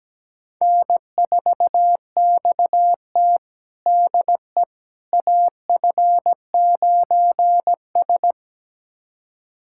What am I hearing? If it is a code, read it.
Morse code, heard as a keyed tone, about 17 wpm, 700 Hz. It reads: N4XT DE AF9S
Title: USA Callsign DE Callsign - 4 Characters - 17wpm